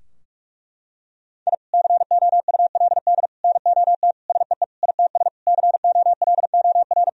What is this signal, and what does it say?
Morse code, heard as a keyed tone, about 46 wpm, 700 Hz. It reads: I COULD NOT SEE ITS COLOR
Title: Sentences from Top 1000 Words - Review Part 3 - 45wpm